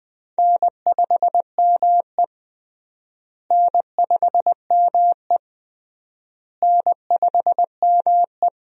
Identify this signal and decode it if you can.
Morse code, heard as a keyed tone, about 20 wpm, 700 Hz. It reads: N5ME N5ME N5ME
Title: USA Call Signs - 4 Characters - Repeated 3x - 20wpm